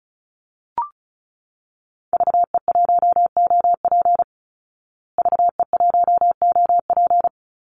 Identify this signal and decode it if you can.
Morse code, heard as a keyed tone, about 34 wpm, 700 Hz. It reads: VE1OP VE1OP
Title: Call Signs Repeated 2x - 35wpm